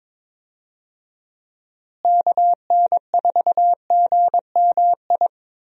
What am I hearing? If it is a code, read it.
Morse code, heard as a keyed tone, about 21 wpm, 700 Hz. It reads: KN4GMI
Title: USA Call Signs - 6 Characters - Repeated 3x - 22wpm